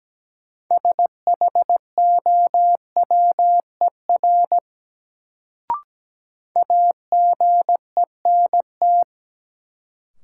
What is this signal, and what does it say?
Morse code, heard as a keyed tone, about 17 wpm, 700 Hz. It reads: SHOWER AGENT
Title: Sentences from Top 1600 Words - New Word Drill - 17wpm